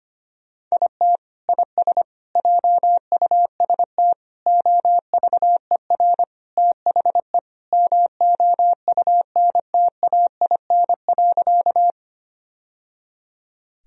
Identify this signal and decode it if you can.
Morse code, heard as a keyed tone, about 25 wpm, 700 Hz. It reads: IT IS JUST OVER THE MOUNTAIN.
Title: Sentences from Top 1000 Words - Review Part 1 - 25wpm